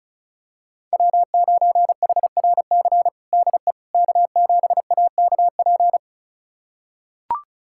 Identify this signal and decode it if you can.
Morse code, heard as a keyed tone, about 36 wpm, 690 Hz. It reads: W9HRC DE K7AKP
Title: USA Callsign DE Callsign - 5 Characters - 35wpm